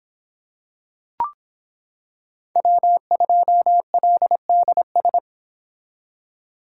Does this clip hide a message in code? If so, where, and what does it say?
Yes: Morse code from 2.6 to 5.2 s, heard as a keyed tone, about 26 wpm, 700 Hz. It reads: W2LDS